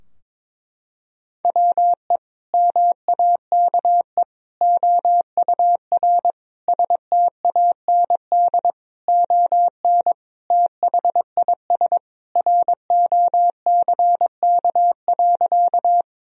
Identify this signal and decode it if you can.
Morse code, heard as a keyed tone, about 21 wpm, 700 Hz. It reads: WE MAKE OUR STAND ON THIS ROCK.
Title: Sentences from Top 1000 Words - Review Part 2 - 22wpm